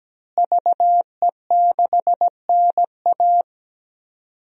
Morse code, heard as a keyed tone, about 17 wpm, 700 Hz. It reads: VE6NA
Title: DXCC-100 Call Signs - 5 Characters - Repeated 2x - 17wpm